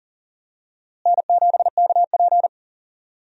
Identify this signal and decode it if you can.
Morse code, heard as a keyed tone, about 40 wpm, 700 Hz. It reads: N7KP
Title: USA Call Signs - 4 Characters - Repeated 2x - 40wpm